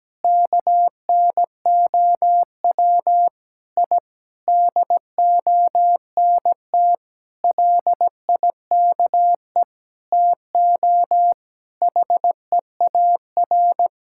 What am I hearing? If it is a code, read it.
Morse code, heard as a keyed tone, about 17 wpm, 700 Hz. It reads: KNOW I DONT LIKE TO HEAR
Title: Sentences from Top 1000 Words - Review Part 6 - 17wpm